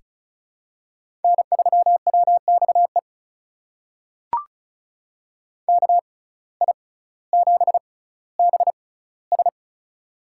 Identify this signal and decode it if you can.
Morse code, heard as a keyed tone, about 36 wpm, spaced as for 18 wpm, 700 Hz. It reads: N3WXE KI7BS
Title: USA Call Signs - 5 Characters - 6x Character Spacing - 35wpm